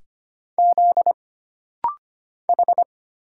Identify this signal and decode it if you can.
Morse code, heard as a keyed tone, about 25 wpm, 700 Hz. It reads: Z H
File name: Single Character (Letter:Number:Punctuation:Prosign) - Rapid-Fire - 25wpm